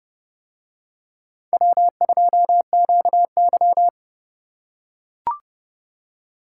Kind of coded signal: Morse code

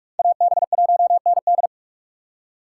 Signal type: Morse code

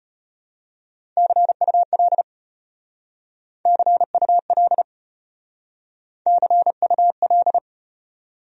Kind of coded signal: Morse code

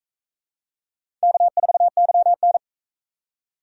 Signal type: Morse code

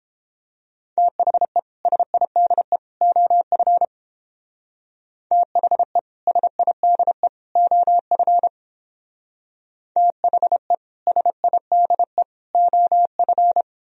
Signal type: Morse code

